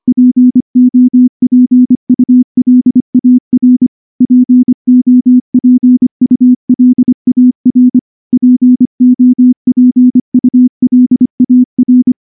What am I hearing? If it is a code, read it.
Morse code, heard as a keyed tone, about 24 wpm, 250 Hz. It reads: POPULAR POPULAR POPULAR
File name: morse-training-25WPM